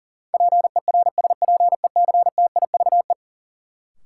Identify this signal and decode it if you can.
Morse code, heard as a keyed tone, about 40 wpm, 700 Hz. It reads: PERSPECTIVE
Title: Sentences from Top 1200 Words - New Word Drill - 40wpm